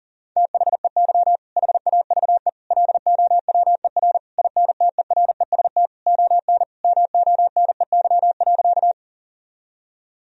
Morse code, heard as a keyed tone, about 40 wpm, 700 Hz. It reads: THEY HAVE LOWER INTEREST ON MONEY.
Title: Sentences from Top 1000 Words - Review Part 4 - 40wpm